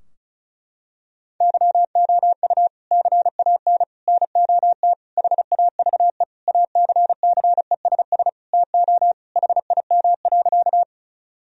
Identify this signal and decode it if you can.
Morse code, heard as a keyed tone, about 36 wpm, 700 Hz. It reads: YOU CAN NOT HAVE ACCESS TO HIM.
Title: Sentences from Top 1000 Words - Review Part 4 - 35wpm